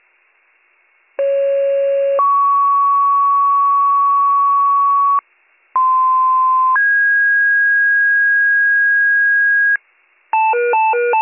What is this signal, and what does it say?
Two-tone page, heard as a one-second tone, then a three-second tone, about 550 then 1120 Hz, about 1040 then 1650 Hz, with noise. It is heard from 1.2 to 9.8 s.